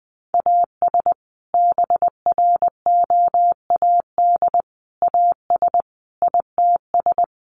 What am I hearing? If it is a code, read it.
Morse code, heard as a keyed tone, about 20 wpm, 700 Hz. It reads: AS BROAD AS ITS
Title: Sentences from Top 1000 Words 20wpm